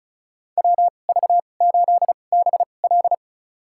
Morse code, heard as a keyed tone, about 36 wpm, spaced as for 25 wpm, 700 Hz. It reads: WV8BL
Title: USA Call Signs - 5 Characters - 2x Character Spacing - 35wpm